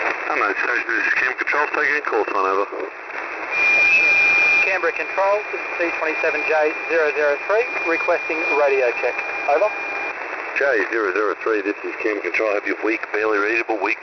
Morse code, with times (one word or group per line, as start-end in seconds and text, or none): none